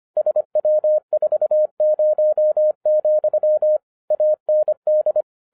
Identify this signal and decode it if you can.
Morse code, heard as a keyed tone, about 26 wpm, 610 Hz. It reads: SW40, AND
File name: 2011-07-mp3 / n24 QRP OTAKU 25wpm